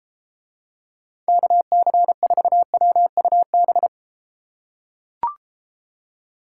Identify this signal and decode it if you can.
Morse code, heard as a keyed tone, about 34 wpm, 700 Hz. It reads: KC4WUB